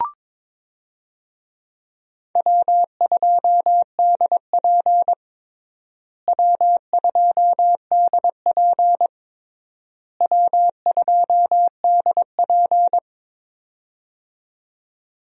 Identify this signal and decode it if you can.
Morse code, heard as a keyed tone, about 22 wpm, 700 Hz. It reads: W2DP W2DP W2DP
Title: USA Call Signs - 4 Characters - Repeated 3x - 22wpm